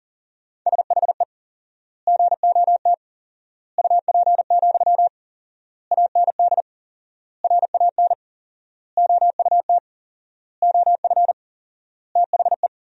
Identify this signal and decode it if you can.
Morse code, heard as a keyed tone, about 40 wpm, 700 Hz. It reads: SHE GOT UP, AND RAN OUT OF THE
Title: 4x Word-Spacing - Sentences from Top 500 Words - 40wpm